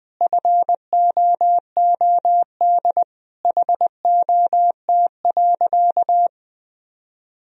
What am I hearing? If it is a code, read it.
Morse code, heard as a keyed tone, about 20 wpm, 700 Hz. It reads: FOOD HOT.
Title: Sentences from Top 1000 Words - Review Part 3 - 20wpm